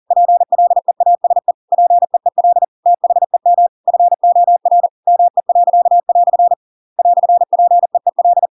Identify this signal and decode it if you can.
Morse code, heard as a keyed tone, about 40 wpm, 700 Hz. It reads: PLEASE PEEL THEM FOR ME." "PEEL
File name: Pinocchio / Pinocchio ch7 40 40 ik1hge